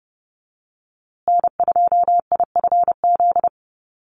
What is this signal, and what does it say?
Morse code, heard as a keyed tone, about 30 wpm, 700 Hz. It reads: N2IFZ
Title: USA Call Signs Repeated 3x 30wpm